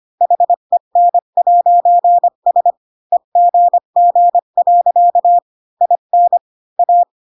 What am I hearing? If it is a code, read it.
Morse code, heard as a keyed tone, about 25 wpm, 700 Hz. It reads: HEN'S EGG. IN A
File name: Pinocchio / Pinocchio ch5 20 25 ik1hge